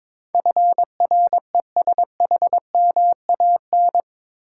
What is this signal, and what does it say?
Morse code, heard as a keyed tone, about 21 wpm, 700 Hz. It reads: FRESHMAN
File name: practice / 22wpm / Eight Letter Words - 22wpm